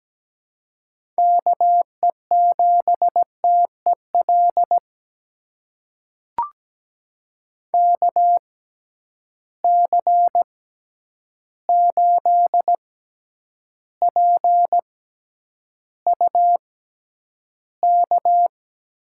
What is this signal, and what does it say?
Morse code, heard as a keyed tone, about 17 wpm, spaced as for 9 wpm, 700 Hz. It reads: KE7TEL KC8PUK
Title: USA Call Signs - 6x Character Spacing - 17wpm